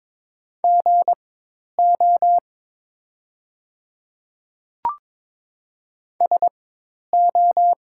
Morse code, heard as a keyed tone, about 22 wpm, spaced as for 10 wpm, 700 Hz. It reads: GO SO
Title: 4x Character Spacing - Top 100 Words - 22wpm